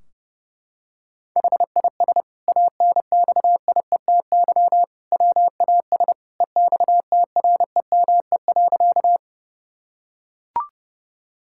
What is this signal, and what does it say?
Morse code, heard as a keyed tone, about 28 wpm, 700 Hz. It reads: HIS ANXIETY WAS EXTREME.
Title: Sentences from Top 1600 Words 30wpm